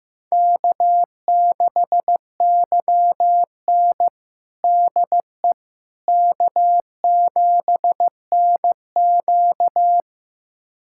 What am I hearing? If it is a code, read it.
Morse code, heard as a keyed tone, about 15 wpm, 700 Hz. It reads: K6YN DE K7NQ
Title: USA Callsign DE Callsign - 4 Characters - 15wpm